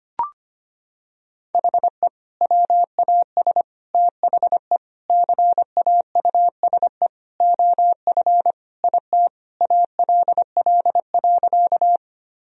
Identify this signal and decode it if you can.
Morse code, heard as a keyed tone, about 25 wpm, 700 Hz. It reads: HE WAS THE CAUSE OF IT ALL.
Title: Sentences from Top 1000 Words - Review Part 1 - 25wpm